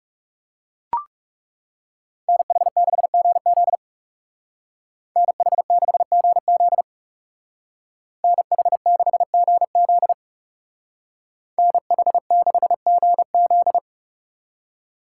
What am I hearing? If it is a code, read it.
Morse code, heard as a keyed tone, about 36 wpm, 700 Hz. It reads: NH6GZ NH6GZ NH6GZ NH6GZ
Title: Speed Racing - USA 2x2 Call Signs 30wpm